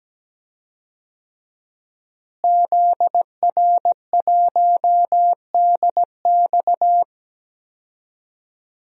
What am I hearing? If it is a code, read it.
Morse code, heard as a keyed tone, about 17 wpm, 700 Hz. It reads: ZR1DX
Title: DXCC-100 Call Signs - 5 Characters - Repeated 3x - 17wpm